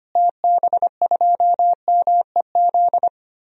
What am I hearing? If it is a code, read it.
Morse code, heard as a keyed tone, about 25 wpm, 700 Hz. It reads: TB2MEZ
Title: DXCC-100 Call Signs - 6 Characters - Repeated 3x - 25wpm